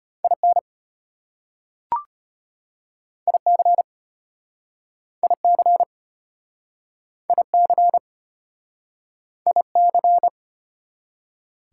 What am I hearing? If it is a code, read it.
Morse code, heard as a keyed tone, about 35 wpm, 690 Hz. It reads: IN IC IC IC IC